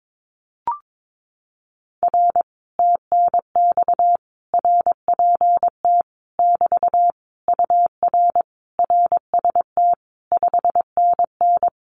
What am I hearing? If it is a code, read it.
Morse code, heard as a keyed tone, about 22 wpm, 700 Hz. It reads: R TNX RPT <BT> UR RST 5NN
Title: Common QSO Phrases Encore 22wpm